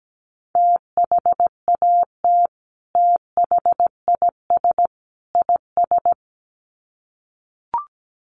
Morse code, heard as a keyed tone, about 17 wpm, 700 Hz. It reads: THAT THIS IS
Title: Sets of 3 Words 17wpm